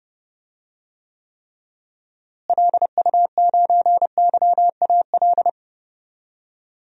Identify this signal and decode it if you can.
Morse code, heard as a keyed tone, about 30 wpm, 700 Hz. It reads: LU9YAL
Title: DXCC-100 Call Signs - 6 Characters - Repeated 3x - 30wpm